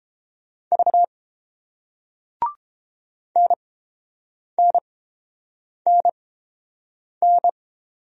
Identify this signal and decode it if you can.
Morse code, heard as a keyed tone, about 30 wpm, spaced as for 7 wpm, 700 Hz. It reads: V NNNN